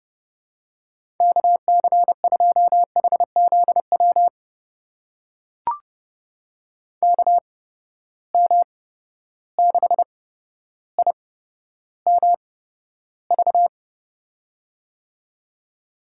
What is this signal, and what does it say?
Morse code, heard as a keyed tone, about 31 wpm, spaced as for 13 wpm, 700 Hz. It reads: KC2HZW KM6IMV
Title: USA Call Signs - 6 Characters - 8x Character Spacing - 30wpm